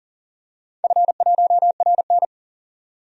Morse code, heard as a keyed tone, about 40 wpm, 700 Hz. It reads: F1RN
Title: DXCC-100 Call Signs - 4 Characters - 40wpm